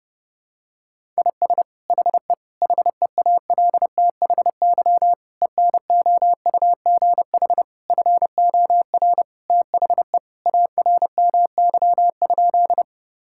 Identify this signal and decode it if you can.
Morse code, heard as a keyed tone, about 30 wpm, 700 Hz. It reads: IS HE HEALTHY ENOUGH FOR THE ARMY?
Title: Sentences from Top 1000 Words - Review Part 5 - 30wpm